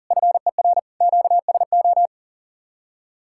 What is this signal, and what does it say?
Morse code, heard as a keyed tone, about 51 wpm, 700 Hz. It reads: FER QSO